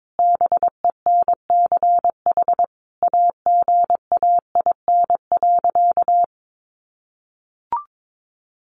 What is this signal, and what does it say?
Morse code, heard as a keyed tone, about 22 wpm, 700 Hz. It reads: BENCH AGAIN.
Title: Sentences from Top 1400 Words 22wpm